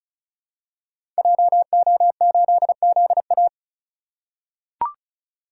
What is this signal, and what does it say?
Morse code, heard as a keyed tone, about 36 wpm, 700 Hz. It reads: JO8ZA
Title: Code Groups - 5 Letter:Numbers - 35wpm